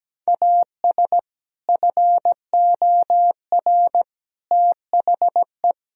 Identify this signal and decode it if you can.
Morse code, heard as a keyed tone, about 17 wpm, 700 Hz. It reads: AS FOR THE
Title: Sentences from Top 1000 Words - Review Part 1 - 17wpm